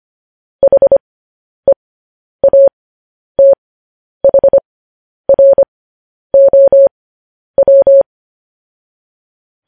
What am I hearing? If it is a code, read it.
Morse code, heard as a keyed tone, about 25 wpm, spaced as for 10 wpm, 550 Hz. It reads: HEATHROW
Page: news headlines in morse code at 10 wpm